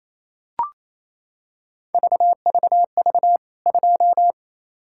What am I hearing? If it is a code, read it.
Morse code, heard as a keyed tone, about 27 wpm, 700 Hz. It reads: VVV 2
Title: K1USN SST QSO Practice 28wpm